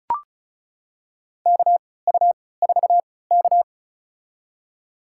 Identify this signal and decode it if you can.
Morse code, heard as a keyed tone, about 36 wpm, spaced as for 20 wpm, 700 Hz. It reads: KU4K
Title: USA Call Signs - 4 Characters - 3x Character Spacing - 35wpm